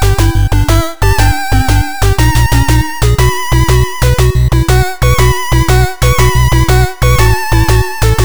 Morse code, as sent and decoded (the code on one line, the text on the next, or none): none